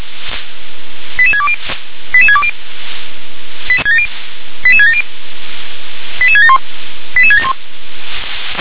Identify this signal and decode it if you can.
Five-tone selective call (ZVEI), heard as quick runs of five tones, tones of about 70 ms. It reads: 88620 88620 88670 88671 88671